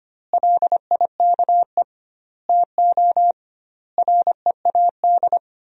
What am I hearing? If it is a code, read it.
Morse code, heard as a keyed tone, about 25 wpm, 700 Hz. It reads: LIKE TO READ
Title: 2x Word-Spacing - Sentences from Top 500 Words - 25wpm